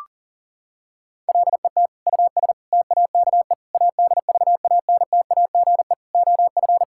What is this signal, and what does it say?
Morse code, heard as a keyed tone, about 37 wpm, 700 Hz. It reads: LET US TAKE ADVANTAGE OF
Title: Sentences from Top 1000 Words - Review Part 6 - 40wpm